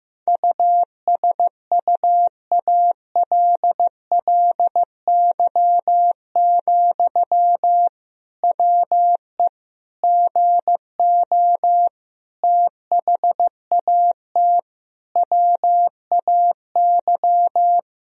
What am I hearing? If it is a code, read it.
Morse code, heard as a keyed tone, about 15 wpm, 700 Hz. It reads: USUALLY, WE GO THAT WAY
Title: Sentences from Top 1000 Words - Review Part 3 - 15wpm